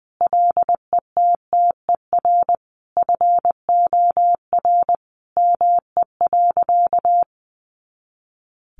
Morse code, heard as a keyed tone, about 20 wpm, 700 Hz. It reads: LETTER FOR ME.
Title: Sentences from Top 300 Words 20wpm